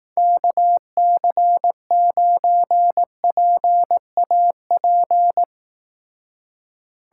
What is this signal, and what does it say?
Morse code, heard as a keyed tone, about 18 wpm, 700 Hz. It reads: KC9PAP
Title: Speed Racing - USA 2x3 Call Signs 15wpm